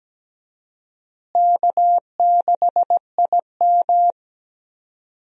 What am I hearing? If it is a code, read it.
Morse code, heard as a keyed tone, about 17 wpm, 700 Hz. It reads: K6IM
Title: USA Call Signs - 4 Characters - Repeated 3x - 17wpm